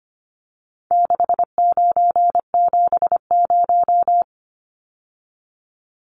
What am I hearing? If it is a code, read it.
Morse code, heard as a keyed tone, about 25 wpm, 700 Hz. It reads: 6970
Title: Numbers - 4-Digit 25wpm